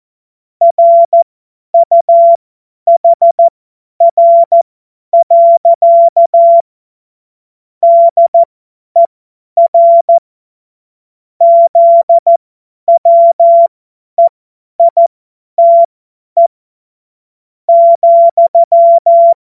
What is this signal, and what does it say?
Morse code, heard as a keyed tone, about 14 wpm, spaced as for 9 wpm, 680 Hz. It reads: RUHR. DER ZWEITE,
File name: Morsekurs / Witze / Witz 26 März 2024 50 F70